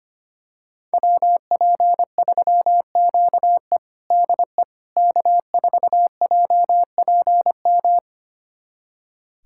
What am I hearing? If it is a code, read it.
Morse code, heard as a keyed tone, about 25 wpm, 700 Hz. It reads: WP3QE DE K4JPM
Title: USA Callsign DE Callsign - 5 Characters - 25wpm